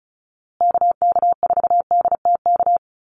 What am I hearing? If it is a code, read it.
Morse code, heard as a keyed tone, about 34 wpm, 700 Hz. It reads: KK4DTK
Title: USA Call Signs Repeated 2x 35wpm